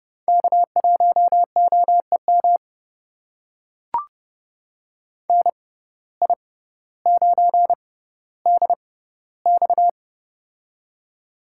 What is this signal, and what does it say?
Morse code, heard as a keyed tone, about 30 wpm, spaced as for 15 wpm, 700 Hz. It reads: K1OEM NI9DX
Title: USA Call Signs - 5 Characters - 6x Character Spacing - 30wpm